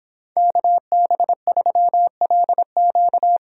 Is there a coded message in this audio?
Yes: Morse code from 0.4 to 3.4 s, heard as a keyed tone, about 26 wpm, 700 Hz. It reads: KB3LQ